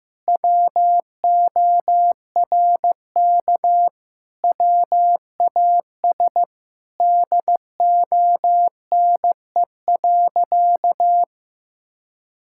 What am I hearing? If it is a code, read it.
Morse code, heard as a keyed tone, about 15 wpm, 700 Hz. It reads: WORK WAS DONE.
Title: Sentences from Top 1000 Words - Review Part 5 - 15wpm